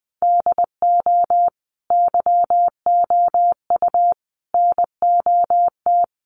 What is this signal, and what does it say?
Morse code, heard as a keyed tone, about 20 wpm, 700 Hz. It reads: DO YOU NOT
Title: Sentences from Top 600 Words 20wpm